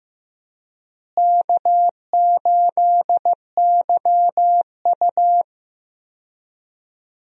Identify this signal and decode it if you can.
Morse code, heard as a keyed tone, about 16 wpm, 700 Hz. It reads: K8YU
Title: USA Call Signs - 4 Characters - Repeated 3x - 15wpm